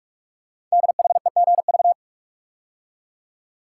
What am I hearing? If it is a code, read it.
Morse code, heard as a keyed tone, about 44 wpm, 700 Hz. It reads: DHEGV
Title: Code Groups - 5 Letters - 45wpm